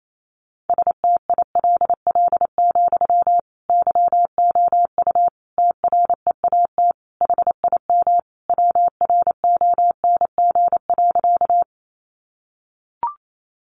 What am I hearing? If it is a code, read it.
Morse code, heard as a keyed tone, about 28 wpm, 700 Hz. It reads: STILL, YOU TREAT HIM WRONG.
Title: Sentences from Top 1300 Words 28wpm